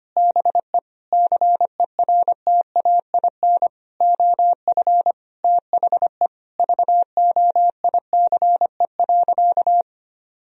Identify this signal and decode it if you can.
Morse code, heard as a keyed tone, about 25 wpm, 700 Hz. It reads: BE CERTAIN OF THE VOICE.
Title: Sentences from Top 1000 Words - Review Part 2 - 25wpm